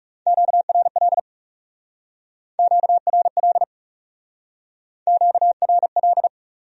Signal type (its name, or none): Morse code